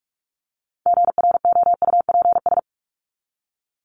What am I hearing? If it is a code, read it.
Morse code, heard as a keyed tone, about 44 wpm, 700 Hz. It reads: GROUPS
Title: ES:S Suffix - Top 100 Words with Base Words 45wpm